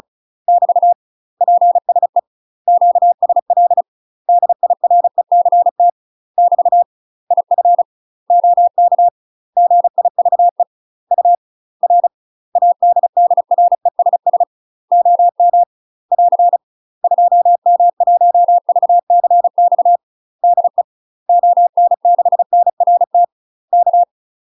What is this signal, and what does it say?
Morse code, heard as a keyed tone, about 36 wpm, 700 Hz. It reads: <BT> PSE QSL DIRECT <BT> IF OK GIVE UR ADDRESS OM <AR> 2M1VCX DE ON6NRT K
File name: UK MASTER QSOs TRAINING / 35WPM / UK MASTER QSOs 0010